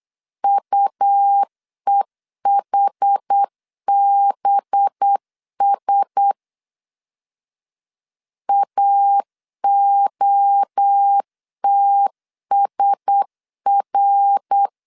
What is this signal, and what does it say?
Morse code, heard as a keyed tone, about 8 wpm, 800 Hz. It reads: UEHBS AOTSR
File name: 8wpm